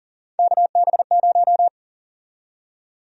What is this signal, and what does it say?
Morse code, heard as a keyed tone, about 38 wpm, 700 Hz. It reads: KB0
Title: USA Call Sign Prefixes - Repeated 2x - 40wpm